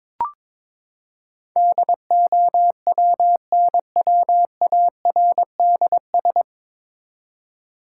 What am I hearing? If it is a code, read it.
Morse code, heard as a keyed tone, about 22 wpm, 700 Hz. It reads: DOWNWARDS
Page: Morse Code Ninja practice 22wpm